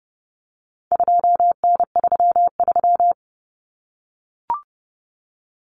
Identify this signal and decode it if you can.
Morse code, heard as a keyed tone, about 30 wpm, 700 Hz. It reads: 2N33
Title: Common Cut Numbers - 4-Digit 30wpm